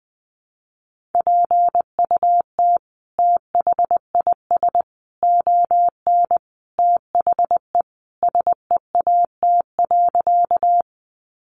Morse code, heard as a keyed tone, about 20 wpm, 700 Hz. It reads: PUT THIS ON THE SEAT.
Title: Sentences from Top 1500 Words 20wpm